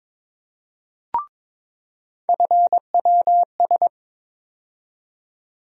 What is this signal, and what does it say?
Morse code, heard as a keyed tone, about 22 wpm, 700 Hz. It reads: FWS
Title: USA Call Sign Postfixes - 3 Letters 22wpm